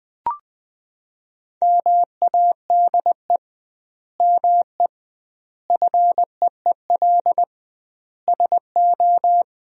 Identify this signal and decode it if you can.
Morse code, heard as a keyed tone, about 20 wpm, 700 Hz. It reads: MADE ME FEEL SO
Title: 2x Word-Spacing - Sentences from Top 400 Words - 20wpm